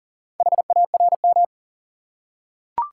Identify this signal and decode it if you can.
Morse code, heard as a keyed tone, about 40 wpm, 700 Hz. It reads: HARM
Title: Four Letter Words - 40wpm